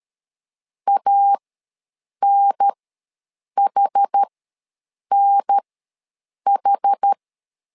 Morse code, heard as a keyed tone, about 13 wpm, spaced as for 7 wpm, 800 Hz. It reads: ANHNH
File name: LEZIONI / 2 UHDN / 2 UHDN ws 12wpm 1708